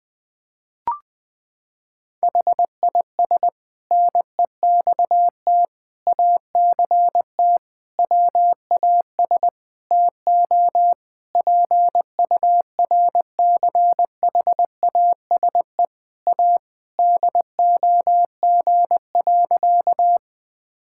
Morse code, heard as a keyed tone, about 20 wpm, 700 Hz. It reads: HIS NEXT ACT WAS TO PURCHASE A DOG.
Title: Sentences from Top 1000 Words - Review Part 5 - 20wpm